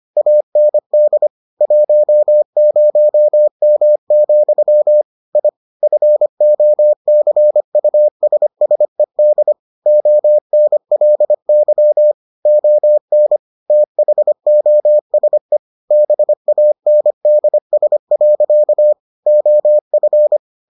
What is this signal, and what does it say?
Morse code, heard as a keyed tone, about 25 wpm, 600 Hz. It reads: AND 10M, I FOCUSSED ONLY ON THOSE BANDS. OF